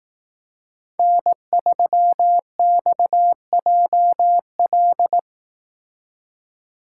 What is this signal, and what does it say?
Morse code, heard as a keyed tone, about 18 wpm, 700 Hz. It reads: N3XJL
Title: Speed Racing - USA 1x3 Call Signs 15wpm